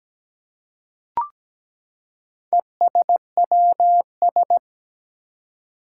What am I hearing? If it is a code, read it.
Morse code, heard as a keyed tone, about 17 wpm, 700 Hz. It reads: ESWS
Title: Four Challenging Letter-Number Combinations 17wpm